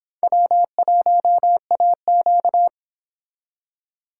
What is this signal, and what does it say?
Morse code, heard as a keyed tone, about 26 wpm, 700 Hz. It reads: W1AQ